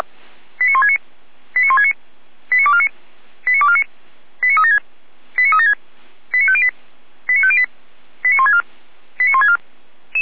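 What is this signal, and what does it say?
Five-tone selective call (ZVEI), heard as quick runs of five tones, tones of about 70 ms. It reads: 89169 89169 89249 89249 89376 89376 89508 89508 89164 89164
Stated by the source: ZVEI five-tone call received by SDR, from the Feuerwehr Orlamünde audio server.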